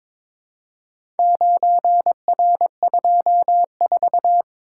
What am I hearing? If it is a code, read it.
Morse code, heard as a keyed tone, about 22 wpm, 700 Hz. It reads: 9R24
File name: practice / 22wpm / Four Letter-Number Combinations - 22wpm